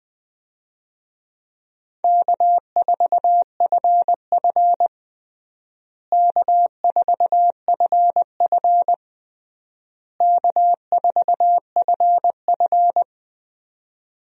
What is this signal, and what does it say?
Morse code, heard as a keyed tone, about 20 wpm, 690 Hz. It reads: K4FF K4FF K4FF
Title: USA Call Signs - 4 Characters - Repeated 3x - 20wpm